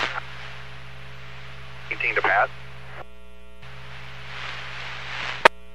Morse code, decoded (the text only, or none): none